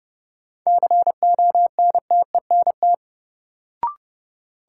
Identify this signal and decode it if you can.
Morse code, heard as a keyed tone, about 30 wpm, 700 Hz. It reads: CONTENT
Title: Sentences from Top 900 Words - New Word Drill - 30wpm